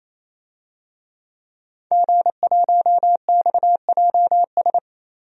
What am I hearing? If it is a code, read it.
Morse code, heard as a keyed tone, about 27 wpm, 700 Hz. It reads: G1XJS
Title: DXCC-100 Call Signs - 5 Characters - 28wpm